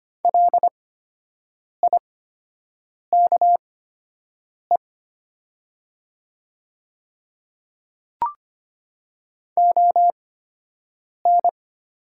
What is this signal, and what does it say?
Morse code, heard as a keyed tone, about 25 wpm, spaced as for 7 wpm, 700 Hz. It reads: LIKE ON
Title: 8x Character Spacing - Top 100 Words - 25wpm